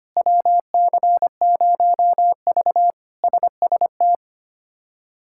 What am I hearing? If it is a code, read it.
Morse code, heard as a keyed tone, about 25 wpm, 700 Hz. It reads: WC0V SST
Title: K1USN SST QSO Practice - 25wpm